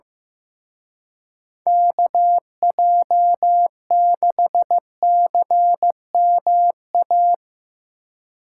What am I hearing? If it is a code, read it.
Morse code, heard as a keyed tone, about 16 wpm, 700 Hz. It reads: KJ6CMA
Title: Speed Racing - USA 2x3 Call Signs 15wpm